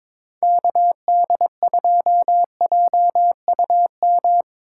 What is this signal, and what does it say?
Morse code, heard as a keyed tone, about 25 wpm, 700 Hz. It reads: KD2JUM
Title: Speed Racing - USA 2x3 Call Signs 22wpm